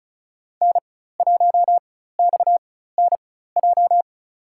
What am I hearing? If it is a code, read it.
Morse code, heard as a keyed tone, about 36 wpm, spaced as for 16 wpm, 700 Hz. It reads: N1XNJ
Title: USA Call Signs - 4x Character Spacing - 35wpm